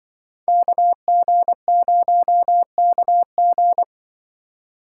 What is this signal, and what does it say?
Morse code, heard as a keyed tone, about 24 wpm, 700 Hz. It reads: KG0KG
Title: Speed Racing - USA 2x2 Call Signs 20wpm